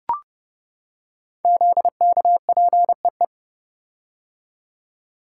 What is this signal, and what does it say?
Morse code, heard as a keyed tone, about 30 wpm, 700 Hz. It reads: ZKPEE
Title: Code Groups - 5 Letters - 30wpm